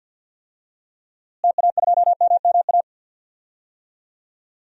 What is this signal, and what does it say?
Morse code, heard as a keyed tone, about 51 wpm, 700 Hz. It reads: TA2MMA
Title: DXCC-100 Call Signs - 6 Characters - Repeated 2x - 50wpm